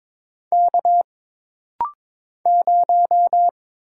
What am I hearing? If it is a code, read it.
Morse code, heard as a keyed tone, about 22 wpm, 700 Hz. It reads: K 0
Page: This rapid-fire timing is geared to push your ICR (Instant Character Recognition) skills.